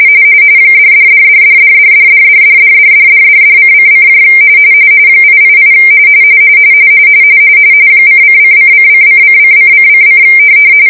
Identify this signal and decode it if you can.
RTTY, about 45.45 baud. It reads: $$$$$$$$$$$$$$$$$::: 888 :::::::::::::